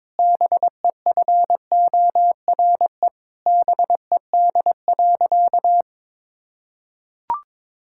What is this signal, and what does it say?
Morse code, heard as a keyed tone, about 22 wpm, 700 Hz. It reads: BEFORE BED.
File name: Difficult Sentences with Hints - 22wpm